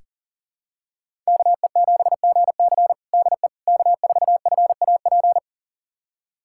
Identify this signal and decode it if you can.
Morse code, heard as a keyed tone, about 40 wpm, 700 Hz. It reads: KE7GC DE K4FAP
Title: USA Callsign DE Callsign 40wpm